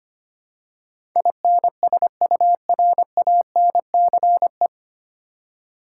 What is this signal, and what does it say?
Morse code, heard as a keyed tone, about 25 wpm, 700 Hz. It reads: INSURANCE
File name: Nine Letter Words - 25wpm